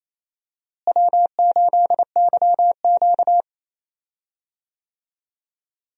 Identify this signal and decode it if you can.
Morse code, heard as a keyed tone, about 28 wpm, 700 Hz. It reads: W8YQ
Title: USA Call Signs - 4 Characters - Repeated 3x - 28wpm